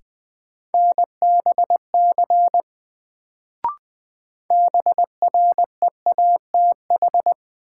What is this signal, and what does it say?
Morse code, heard as a keyed tone, about 20 wpm, 700 Hz. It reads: NBC BREATH
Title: Sets of 5 Words Encore - New Word Drill - 20wpm